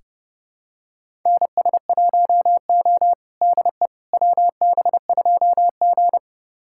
Morse code, heard as a keyed tone, about 30 wpm, 700 Hz. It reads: NS1O DE WB2G
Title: USA Callsign DE Callsign - 4 Characters - 30wpm